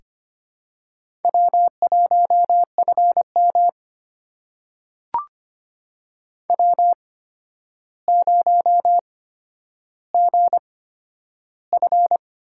Morse code, heard as a keyed tone, about 25 wpm, spaced as for 11 wpm, 700 Hz. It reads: W1FM W0GF
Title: USA Call Signs - 4 Characters - 8x Character Spacing - 25wpm